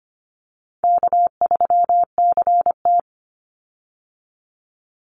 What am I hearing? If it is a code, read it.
Morse code, heard as a keyed tone, about 25 wpm, 700 Hz. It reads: K3CT
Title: Easy Call Signs - 25wpm